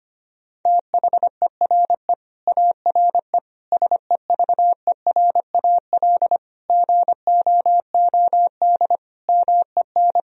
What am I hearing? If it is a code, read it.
Morse code, heard as a keyed tone, about 25 wpm, 700 Hz. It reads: THERE ARE SEVERAL GOOD MEN
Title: Sentences from Top 1000 Words - Review Part 2 - 25wpm